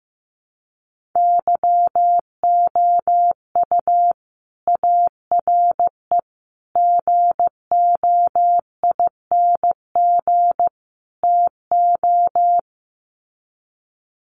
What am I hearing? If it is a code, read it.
Morse code, heard as a keyed tone, about 15 wpm, 700 Hz. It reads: YOU ARE GOING TO
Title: Sets of 4 Words 15wpm